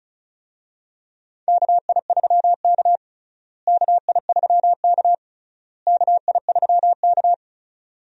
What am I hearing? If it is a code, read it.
Morse code, heard as a keyed tone, about 36 wpm, 700 Hz. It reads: KI3K KI3K KI3K
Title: USA Call Signs - 4 Characters - Repeated 3x - 35wpm